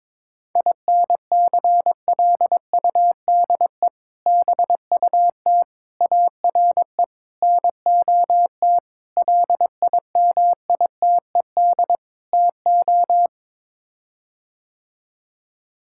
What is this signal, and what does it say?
Morse code, heard as a keyed tone, about 22 wpm, 700 Hz. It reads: INCLUDE BUT ARE NOT LIMITED TO
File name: practice / 22wpm / Sets of 6 Words - 22wpm